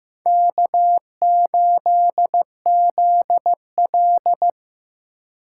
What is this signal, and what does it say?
Morse code, heard as a keyed tone, about 15 wpm, 700 Hz. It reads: K8ZL
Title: USA Call Signs - 4 Characters - 3x Character Spacing - 15wpm